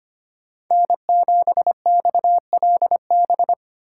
Morse code, heard as a keyed tone, about 25 wpm, 700 Hz. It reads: N7XLB
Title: USA Call Signs - 5 Characters - 4x Character Spacing - 25wpm